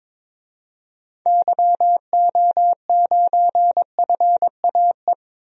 Morse code, heard as a keyed tone, about 22 wpm, 700 Hz. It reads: YO9FAE
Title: DXCC-100 Call Signs - 6 Characters - 22wpm